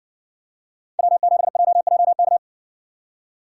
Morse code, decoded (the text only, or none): W7JJK